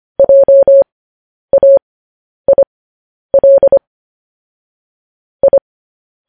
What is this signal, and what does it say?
Morse code, heard as a keyed tone, about 25 wpm, spaced as for 10 wpm, 550 Hz. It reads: JAIL I